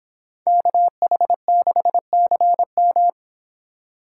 Morse code, heard as a keyed tone, about 25 wpm, 700 Hz. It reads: KH6CM